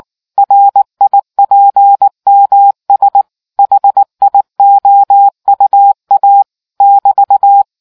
Morse code, heard as a keyed tone, about 19 wpm, 800 Hz. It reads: RIPMS HIOUA <BT>